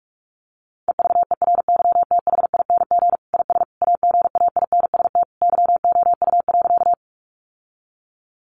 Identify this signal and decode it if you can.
Morse code, heard as a keyed tone, about 44 wpm, 700 Hz. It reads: EVERYTHING IS AGAINST YOU.
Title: Sentences from Top 700 Words 45wpm